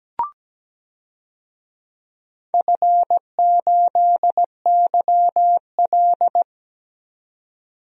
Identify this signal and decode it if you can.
Morse code, heard as a keyed tone, about 17 wpm, 700 Hz. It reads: F8YL
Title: DXCC-100 Call Signs - 4 Characters - 17wpm